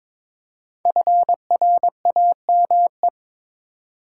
Morse code, heard as a keyed tone, about 22 wpm, 700 Hz. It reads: FRAME
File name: 5 Letter Words - First 3 Letters Given - 22wpm